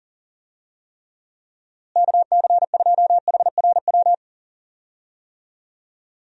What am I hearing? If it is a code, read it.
Morse code, heard as a keyed tone, about 40 wpm, 700 Hz. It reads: KC2HRW
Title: USA Call Signs - 6 Characters - Repeated 3x - 40wpm